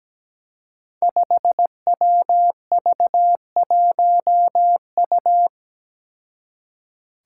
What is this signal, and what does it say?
Morse code, heard as a keyed tone, about 17 wpm, 700 Hz. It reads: 5WV1U
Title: Five Challenging Letter-Number Combinations 17wpm